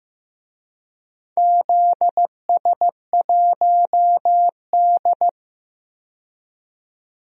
Morse code, heard as a keyed tone, about 15 wpm, 700 Hz. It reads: ZS1D
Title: DXCC-100 Call Signs - 4 Characters - 15wpm